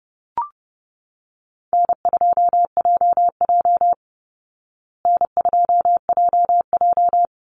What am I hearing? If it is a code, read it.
Morse code, heard as a keyed tone, about 30 wpm, 700 Hz. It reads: N2JJ N2JJ
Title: Call Signs Repeated 3x - 30wpm